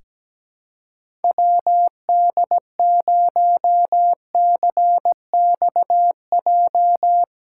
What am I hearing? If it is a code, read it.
Morse code, heard as a keyed tone, about 17 wpm, 700 Hz. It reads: WD0CXJ
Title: USA Callsign DE Callsign - 6 Characters - 17wpm